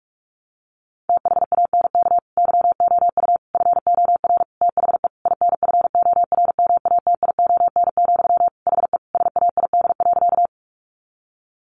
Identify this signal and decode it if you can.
Morse code, heard as a keyed tone, about 44 wpm, 700 Hz. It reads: THANK YOU FOR THE INFORMATION, HE SAID.
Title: Sentences from Top 600 Words 45wpm